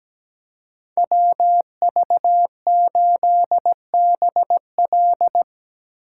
Morse code, heard as a keyed tone, about 17 wpm, 700 Hz. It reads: WV8BL
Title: USA Call Signs - 5 Characters - 3x Character Spacing - 17wpm